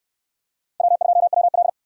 Morse code, heard as a keyed tone, about 59 wpm, 700 Hz. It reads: W2WL